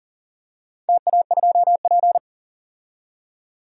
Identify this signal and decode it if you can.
Morse code, heard as a keyed tone, about 40 wpm, 700 Hz. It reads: TA2P
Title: DXCC-100 Call Signs - 4 Characters - 40wpm